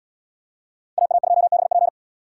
Morse code, heard as a keyed tone, about 77 wpm, 700 Hz. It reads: AA2BP